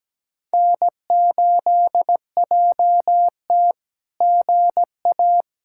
Morse code, heard as a keyed tone, about 17 wpm, 700 Hz. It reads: N8JT GA
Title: SOTA QSO Practice 17wpm